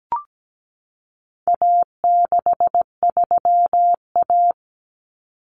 Morse code, heard as a keyed tone, about 17 wpm, 700 Hz. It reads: A63A
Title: Common Cut Numbers - 4-Digit 17wpm